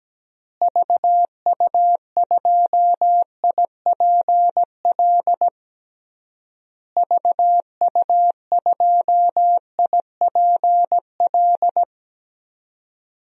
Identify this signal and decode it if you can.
Morse code, heard as a keyed tone, about 17 wpm, 700 Hz. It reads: VU2IPL VU2IPL
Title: DXCC-100 Call Signs - 6 Characters - Repeated 3x - 17wpm